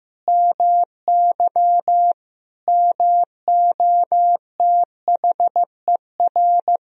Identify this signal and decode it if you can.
Morse code, heard as a keyed tone, about 16 wpm, 700 Hz. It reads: MY MOTHER
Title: Sentences from Top 1000 Words - Review Part 1 - 15wpm